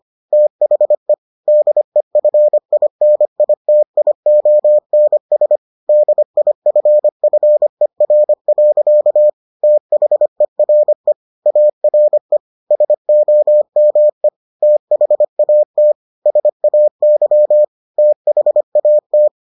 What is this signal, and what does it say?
Morse code, heard as a keyed tone, about 25 wpm, 600 Hz. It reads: THE DEFINITIONS DIFFER. THERE ARE SOME THAT SAY THAT